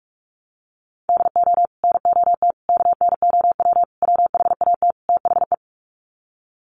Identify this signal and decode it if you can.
Morse code, heard as a keyed tone, about 44 wpm, 700 Hz. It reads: DO NOT KNOW WHAT THE
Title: Sets of 5 Words 45wpm